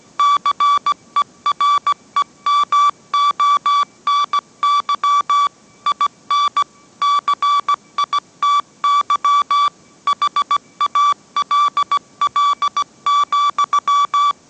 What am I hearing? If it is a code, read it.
Morse code, heard as a keyed tone, about 17 wpm, 1150 Hz. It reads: CEREMONY IN CITY HALL,